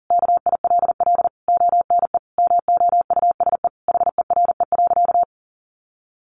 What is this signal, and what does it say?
Morse code, heard as a keyed tone, about 40 wpm, 700 Hz. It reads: KILL ONE MOUSE HERE.
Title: A Sound Of Thunder 40wpm